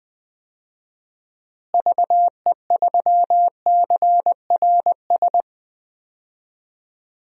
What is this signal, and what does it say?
Morse code, heard as a keyed tone, about 20 wpm, 700 Hz. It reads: VE3CRS
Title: DXCC-100 Call Signs - 6 Characters - Repeated 2x - 20wpm